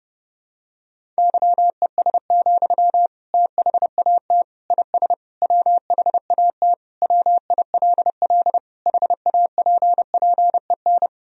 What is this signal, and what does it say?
Morse code, heard as a keyed tone, about 32 wpm, 700 Hz. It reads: YES, THAT IS WHAT WILL HAPPEN
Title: Sentences from Top 500 Words and Abbreviations 30wpm